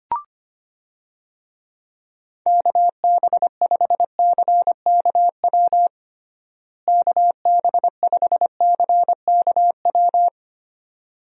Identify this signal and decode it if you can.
Morse code, heard as a keyed tone, about 25 wpm, 700 Hz. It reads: KB5CKW KB5CKW
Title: USA Call Signs - 6 Characters - Repeated 2x - 25wpm